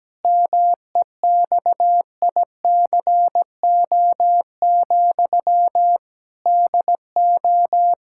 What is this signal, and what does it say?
Morse code, heard as a keyed tone, about 17 wpm, 700 Hz. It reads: MEXICO, DO
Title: Pangram Challenge v2 - 17wpm